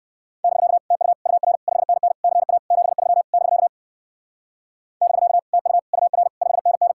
Morse code, heard as a keyed tone, about 64 wpm, 700 Hz. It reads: <BK> TU UR 5NN CA 73 <BK> <BK> TU UR 5NN